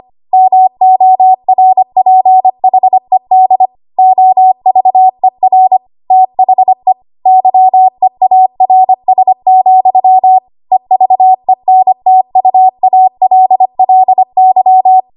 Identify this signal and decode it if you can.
Morse code, heard as a keyed tone, about 25 wpm, 750 Hz. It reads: MORPHED OVER THE YEARS, EVENTUALLY
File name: Archive / 25 WPM / 25WPM